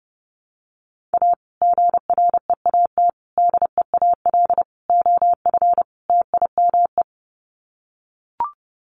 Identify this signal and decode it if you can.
Morse code, heard as a keyed tone, about 30 wpm, 700 Hz. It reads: A GREAT DEAL OF TIME
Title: Sets of 5 Words Encore2 30wpm